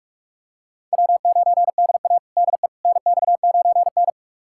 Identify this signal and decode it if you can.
Morse code, heard as a keyed tone, about 46 wpm, 700 Hz. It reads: W9DA DE NX9N
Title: USA Callsign DE Callsign - 4 Characters - 45wpm